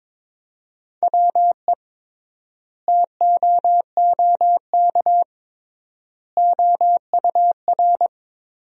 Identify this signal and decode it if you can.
Morse code, heard as a keyed tone, about 22 wpm, 700 Hz. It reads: WE TOOK OUR
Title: 3x Word-Spacing - Sentences from Top 400 Words - 22wpm